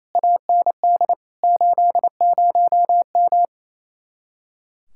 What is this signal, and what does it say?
Morse code, heard as a keyed tone, about 27 wpm, 700 Hz. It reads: AND 80M
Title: Speed-Racing Common QSO Phrases Encore - 28wpm